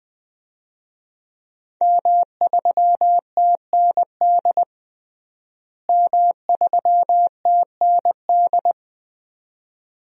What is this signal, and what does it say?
Morse code, heard as a keyed tone, about 20 wpm, 700 Hz. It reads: M3TND M3TND
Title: DXCC-100 Call Signs - 5 Characters - Repeated 2x - 20wpm